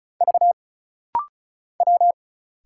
Morse code, heard as a keyed tone, about 34 wpm, 700 Hz. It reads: V W
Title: Single Letters - Rapid-Fire 35wpm